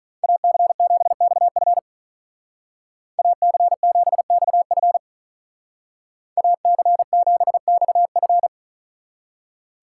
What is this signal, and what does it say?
Morse code, heard as a keyed tone, about 41 wpm, 700 Hz. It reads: AC7XF AC7XF AC7XF
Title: Speed Racing - USA Call Signs 35wpm